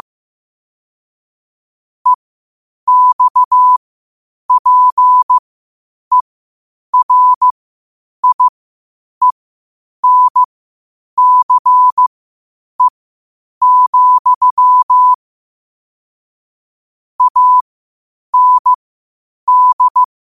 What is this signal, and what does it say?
Morse code, heard as a keyed tone, about 15 wpm, spaced as for 8 wpm, 1000 Hz. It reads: EXPERIENCE, AND
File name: qotdfiles / 8wpm